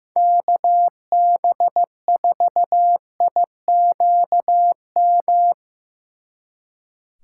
Morse code, heard as a keyed tone, about 17 wpm, 700 Hz. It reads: KB4IQM
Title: Speed Racing - USA 2x3 Call Signs 15wpm